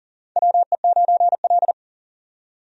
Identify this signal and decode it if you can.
Morse code, heard as a keyed tone, about 40 wpm, 700 Hz. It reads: WE9L
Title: USA Call Signs - 4 Characters - 40wpm